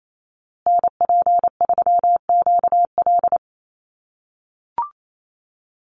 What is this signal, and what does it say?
Morse code, heard as a keyed tone, about 28 wpm, 700 Hz. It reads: NP3QL
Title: USA Call Signs Repeated 3x 28wpm